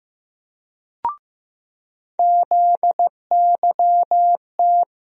Morse code, heard as a keyed tone, about 15 wpm, 700 Hz. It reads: ZYT
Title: USA Call Sign Postfixes - 3 Letters 15wpm